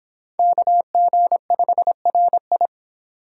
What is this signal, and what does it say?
Morse code, heard as a keyed tone, about 31 wpm, 700 Hz. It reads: KG5RI